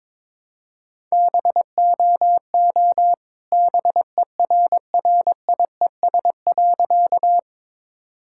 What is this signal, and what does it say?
Morse code, heard as a keyed tone, about 22 wpm, 690 Hz. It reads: BOO BERRIES.
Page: morse-code-ninja-practice-22wpm